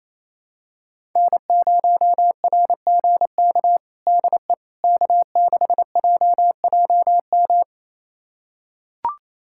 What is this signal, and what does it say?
Morse code, heard as a keyed tone, about 27 wpm, 700 Hz. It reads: N0RGK DE K6JJM
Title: USA Callsign DE Callsign - 5 Characters - 28wpm